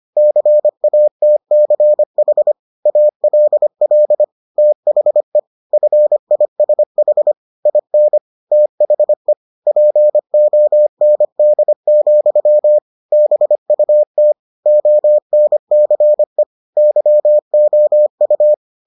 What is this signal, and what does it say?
Morse code, heard as a keyed tone, about 25 wpm, 600 Hz. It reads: CATCH ALL THE FISH IN THE POND, BUT ONCE YOU